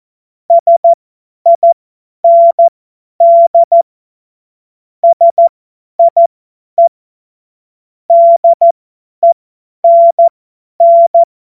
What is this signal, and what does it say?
Morse code, heard as a keyed tone, about 14 wpm, spaced as for 10 wpm, 680 Hz. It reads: SIND SIE DENN
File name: Witz-02-September-2025-50-F70